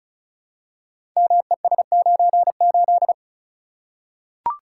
Morse code, heard as a keyed tone, about 36 wpm, 700 Hz. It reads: MES98
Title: Code Groups - 5 Letter:Numbers - 35wpm